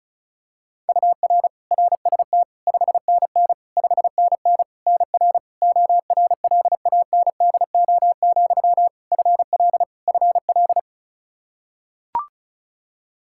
Morse code, heard as a keyed tone, about 34 wpm, 700 Hz. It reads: UR RST 5NN 5NN NR ORLANDO, FL FL
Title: Practice QSO - Exchange 1 - Beginning - 35wpm